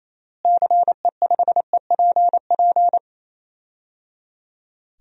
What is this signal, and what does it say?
Morse code, heard as a keyed tone, about 28 wpm, 700 Hz. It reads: CE5EPP
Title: DXCC-100 Call Signs - 6 Characters - Repeated 3x - 28wpm